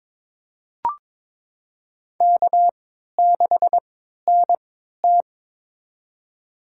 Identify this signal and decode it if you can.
Morse code, heard as a keyed tone, about 22 wpm, spaced as for 13 wpm, 700 Hz. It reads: K6NT
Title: USA Call Signs - 4 Characters - 3x Character Spacing - 22wpm